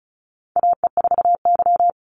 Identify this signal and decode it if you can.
Morse code, heard as a keyed tone, about 34 wpm, 700 Hz. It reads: AE4Y
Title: NA SOTA Chaser Callsigns 35wpm